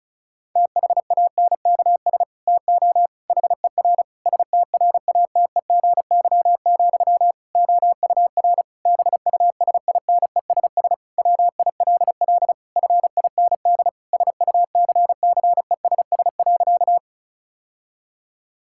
Morse code, heard as a keyed tone, about 36 wpm, 700 Hz. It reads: THANKS TO HER STRATEGY, OUR BUSINESS WILL FIND SUCCESS.
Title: Sentences from Top 1000 Words - Review Part 5 - 35wpm